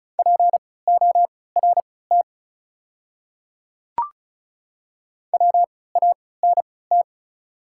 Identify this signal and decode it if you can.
Morse code, heard as a keyed tone, about 36 wpm, spaced as for 20 wpm, 690 Hz. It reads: PORT WANT
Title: 3x Character Spacing - Top 500 Words - 35wpm